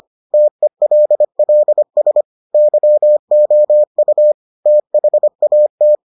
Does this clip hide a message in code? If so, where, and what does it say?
Yes: Morse code from 0.3 to 6.0 s, heard as a keyed tone, about 25 wpm, 600 Hz. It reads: TELLS YOU THAT